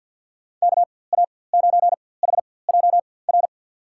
Morse code, heard as a keyed tone, about 53 wpm, spaced as for 24 wpm, 700 Hz. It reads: KA9HJR
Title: USA Call Signs - 6 Characters - 4x Character Spacing - 50wpm